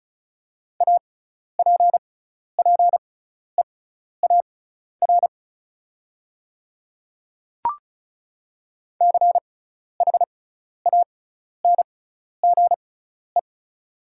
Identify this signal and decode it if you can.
Morse code, heard as a keyed tone, about 36 wpm, spaced as for 12 wpm, 690 Hz. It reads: APPEAR CHANGE
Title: 6x Character Spacing - Top 500 Words - 35wpm